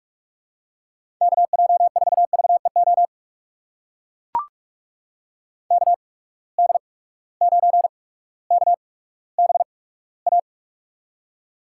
Morse code, heard as a keyed tone, about 44 wpm, spaced as for 19 wpm, 700 Hz. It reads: KJ4VEO KD9KBA
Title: USA Call Signs - 8x Character Spacing - 45wpm